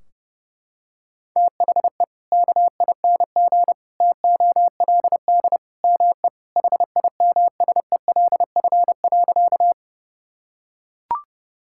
Morse code, heard as a keyed tone, about 30 wpm, 700 Hz. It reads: THE KING TOLD ME HIMSELF.
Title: Sentences from Top 1600 Words 30wpm